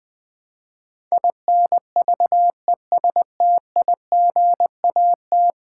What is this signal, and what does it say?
Morse code, heard as a keyed tone, about 20 wpm, 700 Hz. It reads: INVESTIGAT
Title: Build A Word - 5 Letters At A Time - 12-Letter Words - 20wpm